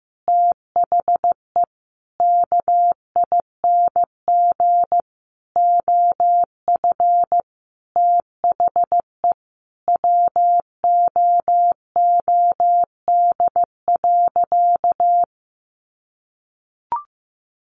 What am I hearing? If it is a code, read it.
Morse code, heard as a keyed tone, about 15 wpm, 700 Hz. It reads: THE KING OF THE WOOD.
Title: Sentences from Top 1200 Words 15wpm